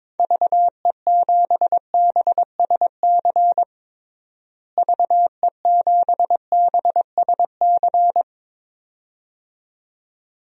Morse code, heard as a keyed tone, about 22 wpm, 700 Hz. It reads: VE7BSC VE7BSC
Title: DXCC-100 Call Signs - 6 Characters - Repeated 2x - 22wpm